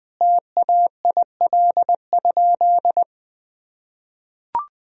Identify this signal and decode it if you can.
Morse code, heard as a keyed tone, about 20 wpm, 700 Hz. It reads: TAIL?
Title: Sentences from Top 1000 Words - Review Part 3 - 20wpm